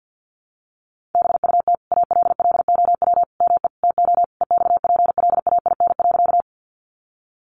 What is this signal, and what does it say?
Morse code, heard as a keyed tone, about 50 wpm, 700 Hz. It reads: BUT ALLOW ME TO EXPLAIN.
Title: Sentences from Top 1100 Words 50wpm